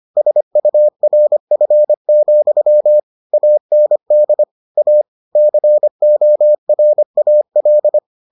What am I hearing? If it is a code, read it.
Morse code, heard as a keyed tone, about 25 wpm, 600 Hz. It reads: SURF, AND A CORAL